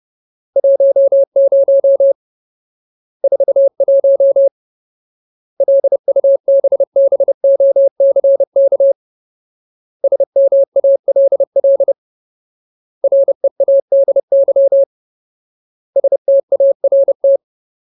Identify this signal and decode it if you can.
Morse code, heard as a keyed tone, about 30 wpm, 550 Hz. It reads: 10 41 LUBBOCK SMALL READY START
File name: Ultimate Word List / 30WPM Ultimate